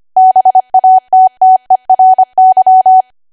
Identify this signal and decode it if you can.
Morse code, heard as a keyed tone, about 25 wpm, 750 Hz. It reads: BATTERY
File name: Archive / 25 WPM / 25WPM